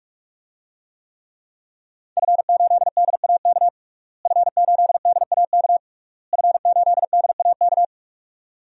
Morse code, heard as a keyed tone, about 46 wpm, 700 Hz. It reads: F8DAK F8DAK F8DAK
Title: DXCC-100 Call Signs - 5 Characters - Repeated 3x - 45wpm